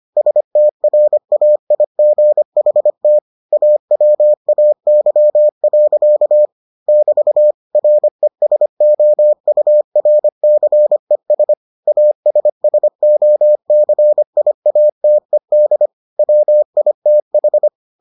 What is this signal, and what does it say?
Morse code, heard as a keyed tone, about 25 wpm, 600 Hz. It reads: STRAIGHT AWAY. <BT> RESOURCES ASSOCIATED WITH